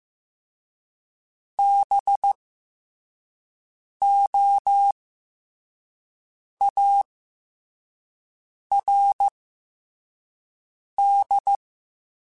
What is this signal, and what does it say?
Morse code, heard as a keyed tone, about 15 wpm, 800 Hz. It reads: B O A R D
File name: Special INGs-mp3s / Special INGs-mp3s / Fast ing @13